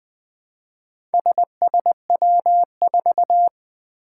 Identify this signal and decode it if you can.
Morse code, heard as a keyed tone, about 20 wpm, 700 Hz. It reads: SSW4
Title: Four Challenging Letter-Number Combinations 20wpm